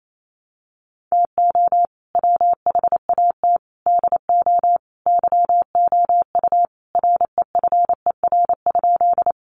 Morse code, heard as a keyed tone, about 28 wpm, 700 Hz. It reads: TO WHAT DO YOU REFER?
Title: Sentences from Top 1000 Words 28wpm